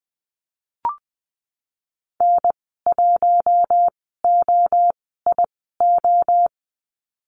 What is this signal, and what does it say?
Morse code, heard as a keyed tone, about 20 wpm, 700 Hz. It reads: N 1 O I O
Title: 062 Lesson 11g - Letter h - Call Signs 20wpm